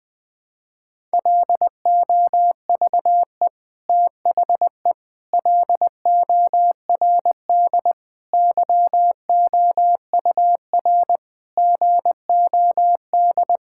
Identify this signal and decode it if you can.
Morse code, heard as a keyed tone, about 20 wpm, 700 Hz. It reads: LOVE THE LORD YOUR GOD
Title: Sets of 9 Words - 20wpm